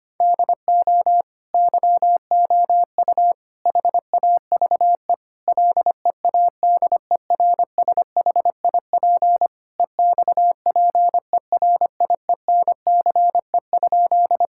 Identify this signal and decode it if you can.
Morse code, heard as a keyed tone, about 25 wpm, 700 Hz. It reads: DO YOU HAVE LEADERSHIP EXPERIENCE?
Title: Sentences from Top 1600 Words 25wpm